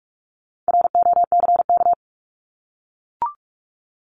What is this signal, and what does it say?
Morse code, heard as a keyed tone, about 44 wpm, 700 Hz. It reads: ROCK
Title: Top 500 Words Repeated 45wpm